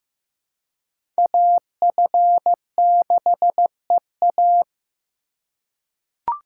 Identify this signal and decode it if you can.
Morse code, heard as a keyed tone, about 15 wpm, 700 Hz. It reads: AF6EA
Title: USA Call Signs - 5 Characters - 2x Character Spacing - 15wpm